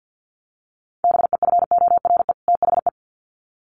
Morse code, heard as a keyed tone, about 50 wpm, 700 Hz. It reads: BEFORE THE
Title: Sets of 2 Words 50wpm